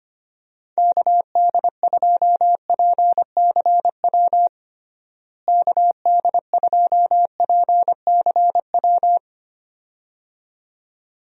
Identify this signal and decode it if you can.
Morse code, heard as a keyed tone, about 25 wpm, 700 Hz. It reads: KD2PCW KD2PCW
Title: USA Call Signs - 6 Characters - Repeated 2x - 25wpm